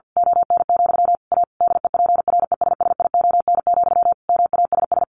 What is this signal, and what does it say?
Morse code, heard as a keyed tone, about 48 wpm, 700 Hz. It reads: ON, A DEPRESSION, MASS
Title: A Sound Of Thunder 50wpm